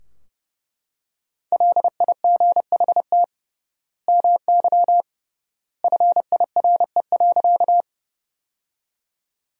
Morse code, heard as a keyed tone, about 30 wpm, 700 Hz. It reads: LIGHT MY FIRE.
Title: 3x Word-Spacing - Sentences from Top 400 Words - 30wpm